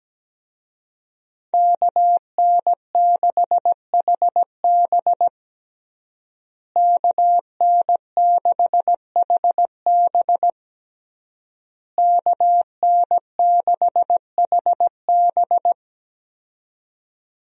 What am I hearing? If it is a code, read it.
Morse code, heard as a keyed tone, about 17 wpm, 700 Hz. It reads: KN6HB KN6HB KN6HB
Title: USA Call Signs - 5 Characters - Repeated 3x - 17wpm